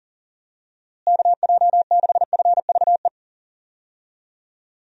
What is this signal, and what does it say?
Morse code, heard as a keyed tone, about 40 wpm, 700 Hz. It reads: KJ6FVE
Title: USA Call Signs - 6 Characters - Repeated 3x - 40wpm